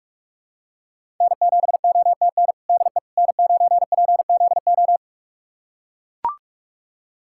Morse code, heard as a keyed tone, about 43 wpm, 700 Hz. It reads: N7OTN DE N9PZO
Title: USA Callsign DE Callsign - 5 Characters - 45wpm